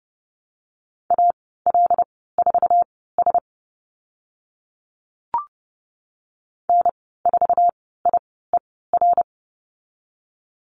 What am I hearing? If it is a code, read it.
Morse code, heard as a keyed tone, about 30 wpm, spaced as for 17 wpm, 700 Hz. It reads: AL4S N4IER
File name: Lesson 14g - Number 2 - Call Signs 30wpm